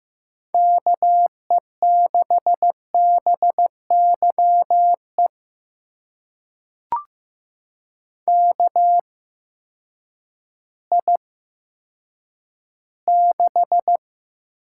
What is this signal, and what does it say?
Morse code, heard as a keyed tone, about 15 wpm, spaced as for 9 wpm, 700 Hz. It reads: KE6BYE KI6
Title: USA Call Signs - 6 Characters - 8x Character Spacing - 15wpm